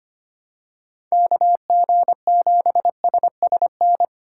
Morse code, heard as a keyed tone, about 25 wpm, 700 Hz. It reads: KG7SSN
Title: USA Call Signs - 6 Characters - 3x Character Spacing - 25wpm